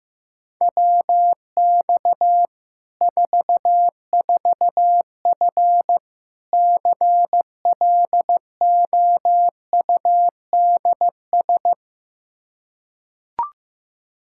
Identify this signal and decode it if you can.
Morse code, heard as a keyed tone, about 16 wpm, 700 Hz. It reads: WX 44F CLOUDS
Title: K1USN SST QSO Practice 15wpm